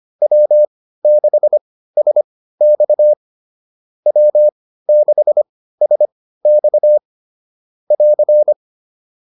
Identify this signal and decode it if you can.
Morse code, heard as a keyed tone, about 25 wpm, spaced as for 15 wpm, 600 Hz. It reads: W6SX W6SX <AR>